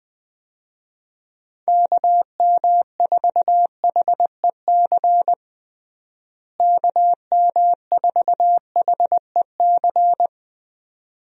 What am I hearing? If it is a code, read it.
Morse code, heard as a keyed tone, about 21 wpm, 700 Hz. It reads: KM4HEC KM4HEC
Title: USA Call Signs - 6 Characters - Repeated 2x - 20wpm